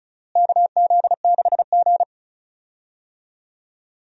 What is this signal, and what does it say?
Morse code, heard as a keyed tone, about 34 wpm, 700 Hz. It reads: KZ6G
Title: USA Call Signs - 4 Characters - Repeated 2x - 35wpm